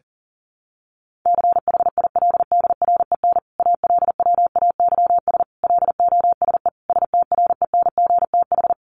Morse code, heard as a keyed tone, about 40 wpm, 700 Hz. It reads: CHILDREN ALWAYS LOSE STRENGTH
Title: Sentences from Top 1300 Words 40wpm